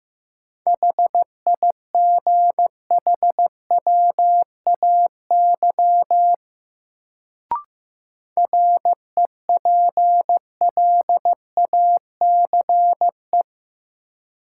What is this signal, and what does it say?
Morse code, heard as a keyed tone, about 15 wpm, 700 Hz. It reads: HIGHWAY REPLACE
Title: Sentences from Top 1400 Words - New Word Drill Alt - 15wpm